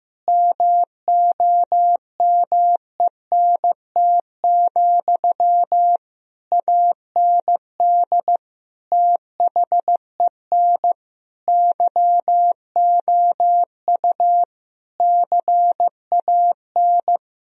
Morse code, heard as a keyed tone, about 15 wpm, 700 Hz. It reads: MOMENT, AND THEN YOU CAN
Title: Sentences from Top 1000 Words - Review Part 4 - 15wpm